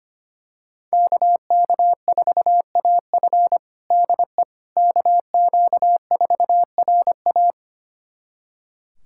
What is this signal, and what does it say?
Morse code, heard as a keyed tone, about 25 wpm, 700 Hz. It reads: KK4AF DE KQ4RA
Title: USA Callsign DE Callsign - 5 Characters - 25wpm